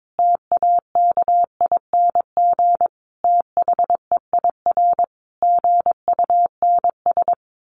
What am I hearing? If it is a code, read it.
Morse code, heard as a keyed tone, about 22 wpm, 700 Hz. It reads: TAKING THEIR GUNS
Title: A Sound Of Thunder 22wpm